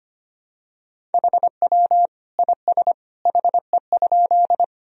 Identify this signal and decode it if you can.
Morse code, heard as a keyed tone, about 25 wpm, 700 Hz. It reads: HW IS HE?
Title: Sentences from Top 100 Words and Abbreviations 25wpm